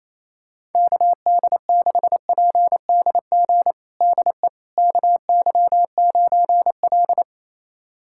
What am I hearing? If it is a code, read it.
Morse code, heard as a keyed tone, about 28 wpm, 700 Hz. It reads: KD6PDG DE KY9L
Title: USA Callsign DE Callsign 28wpm